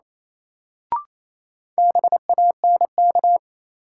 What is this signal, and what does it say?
Morse code, heard as a keyed tone, about 28 wpm, 700 Hz. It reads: BANK
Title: Sets of 5 Words - New Word Drill - 28wpm